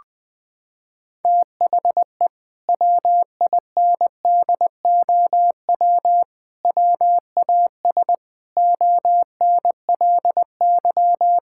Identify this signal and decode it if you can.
Morse code, heard as a keyed tone, about 20 wpm, 700 Hz. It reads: THE WINDOW WAS ONLY
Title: Sentences from Top 1600 Words 20wpm